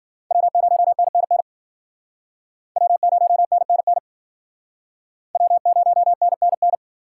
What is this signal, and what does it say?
Morse code, heard as a keyed tone, about 54 wpm, 700 Hz. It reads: W0NNN W0NNN W0NNN